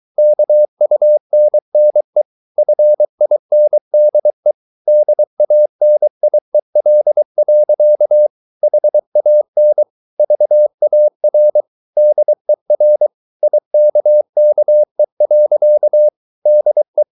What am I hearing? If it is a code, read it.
Morse code, heard as a keyed tone, about 23 wpm, 600 Hz. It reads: KUNNE FINDE DANIEL. HAN VAR DER IKKE. DE
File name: Ferie QRS 23wpm